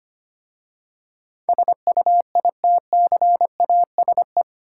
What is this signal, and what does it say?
Morse code, heard as a keyed tone, about 25 wpm, 700 Hz. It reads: SUITCASE
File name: practice / 25wpm / Easy Words with Hints - 25wpm